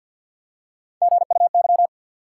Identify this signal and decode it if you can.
Morse code, heard as a keyed tone, about 48 wpm, 700 Hz. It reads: GUY
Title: Three Letter Words - 50wpm